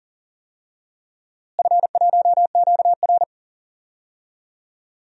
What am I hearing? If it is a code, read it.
Morse code, heard as a keyed tone, about 40 wpm, 700 Hz. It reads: F1QR
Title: DXCC-100 Call Signs - 4 Characters - Repeated 3x - 40wpm